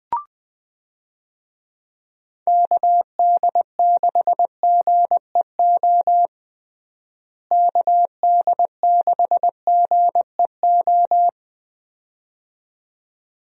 Morse code, heard as a keyed tone, about 20 wpm, 700 Hz. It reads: KD6GEO KD6GEO
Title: USA Call Signs - 6 Characters - Repeated 2x - 20wpm